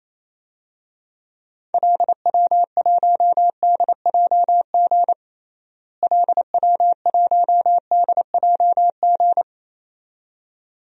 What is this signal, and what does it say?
Morse code, heard as a keyed tone, about 28 wpm, 700 Hz. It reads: LW1DJG LW1DJG
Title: DXCC-100 Call Signs - 6 Characters - Repeated 2x - 28wpm